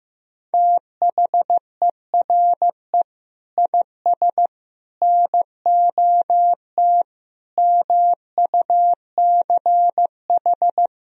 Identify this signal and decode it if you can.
Morse code, heard as a keyed tone, about 15 wpm, 700 Hz. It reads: THERE IS NOT MUCH
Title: Sentences from Top 1000 Words - Review Part 4 - 15wpm